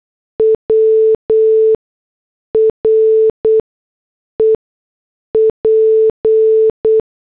Morse code, heard as a keyed tone, about 8 wpm, spaced as for 6 wpm, 440 Hz. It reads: W REP